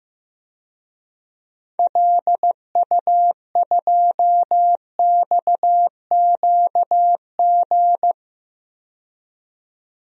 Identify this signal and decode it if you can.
Morse code, heard as a keyed tone, about 15 wpm, 700 Hz. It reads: LU2XQG
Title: DXCC-100 Call Signs - 6 Characters - 15wpm